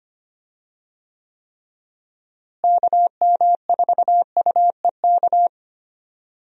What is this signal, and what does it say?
Morse code, heard as a keyed tone, about 25 wpm, 700 Hz. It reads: KM4UEK
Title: USA Call Signs - 6 Characters - Repeated 3x - 25wpm